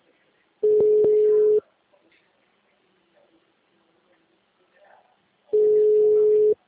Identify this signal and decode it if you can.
Ringback tone (Europe, 425 Hz), with speech.
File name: out- -2009- - - .47740